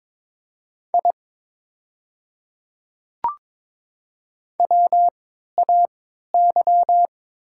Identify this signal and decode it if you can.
Morse code, heard as a keyed tone, about 22 wpm, spaced as for 12 wpm, 700 Hz. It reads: I WAY
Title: 3x Character Spacing - Top 100 Words - 22wpm